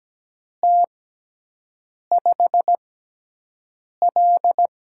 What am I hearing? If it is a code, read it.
Morse code, heard as a keyed tone, about 17 wpm, spaced as for 6 wpm, 700 Hz. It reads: T5L
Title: USA Call Signs - 6x Character Spacing - 17wpm